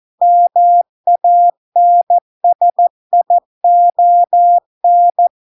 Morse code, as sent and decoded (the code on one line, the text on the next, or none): -- .- -. ... .. --- -.
MANSION